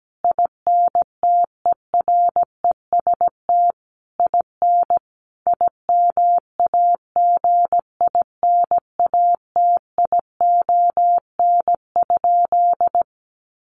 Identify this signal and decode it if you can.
Morse code, heard as a keyed tone, about 17 wpm, 700 Hz. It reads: INTEREST IN IMAGINATION?
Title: Sentences from Top 1300 Words 17wpm